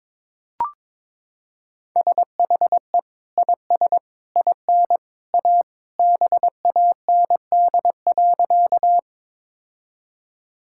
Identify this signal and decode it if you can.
Morse code, heard as a keyed tone, about 21 wpm, 700 Hz. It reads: SHE IS IN A BAND.
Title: Sentences from Top 1600 Words 22wpm